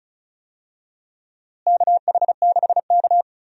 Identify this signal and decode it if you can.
Morse code, heard as a keyed tone, about 36 wpm, 700 Hz. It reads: KH6K
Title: DXCC-100 Call Signs - 4 Characters - Repeated 3x - 35wpm